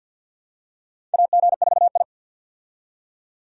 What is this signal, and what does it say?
Morse code, heard as a keyed tone, about 50 wpm, 700 Hz. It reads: AG4I
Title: USA Call Signs - 4 Characters - 50wpm